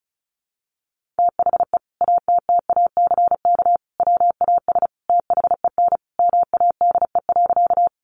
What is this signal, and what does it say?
Morse code, heard as a keyed tone, about 34 wpm, 700 Hz. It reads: THE ATTACK WAS THEN MADE.
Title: Sentences from Top 1500 Words 35wpm